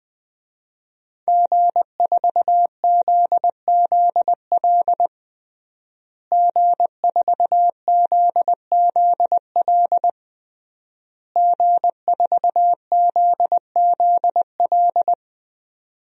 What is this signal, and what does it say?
Morse code, heard as a keyed tone, about 20 wpm, 700 Hz. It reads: G4ZZL G4ZZL G4ZZL
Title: DXCC-100 Call Signs - 5 Characters - Repeated 3x - 20wpm